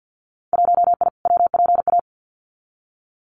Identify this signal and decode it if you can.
Morse code, heard as a keyed tone, about 50 wpm, 700 Hz. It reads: 1I WPA
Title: Winter Field Day Exchanges Encore 50wpm